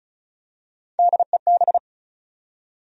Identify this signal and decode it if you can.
Morse code, heard as a keyed tone, about 36 wpm, 700 Hz. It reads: DEB
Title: USA Common Names - 35wpm